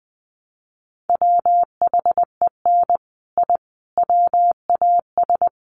Morse code, heard as a keyed tone, about 20 wpm, 700 Hz. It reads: WHEN I WAS
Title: Sets of 5 Words Encore 20wpm